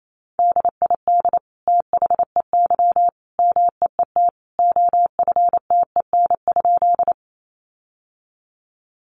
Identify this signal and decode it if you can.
Morse code, heard as a keyed tone, about 28 wpm, 700 Hz. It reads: DID THEY MEET OFTEN?
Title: Sentences from Top 1100 Words 28wpm